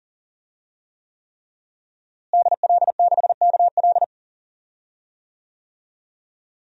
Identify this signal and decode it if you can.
Morse code, heard as a keyed tone, about 40 wpm, 700 Hz. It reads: DL6KL
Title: DXCC-100 Call Signs - 5 Characters - Repeated 3x - 40wpm